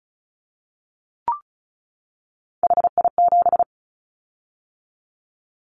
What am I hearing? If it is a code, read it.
Morse code, heard as a keyed tone, about 34 wpm, 700 Hz. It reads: HI7
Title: Callsign Prefixes for Top 100 DXCC - 35wpm